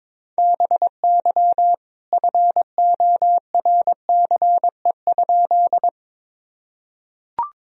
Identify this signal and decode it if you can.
Morse code, heard as a keyed tone, about 22 wpm, 700 Hz. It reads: BY FORCE?
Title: Sentences from Top 1000 Words - Review Part 3 - 22wpm